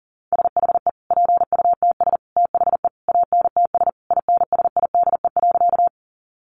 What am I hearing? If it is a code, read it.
Morse code, heard as a keyed tone, about 40 wpm, 700 Hz. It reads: SHE PUTS THE ANTS INSIDE.
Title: Easy Phrases 40wpm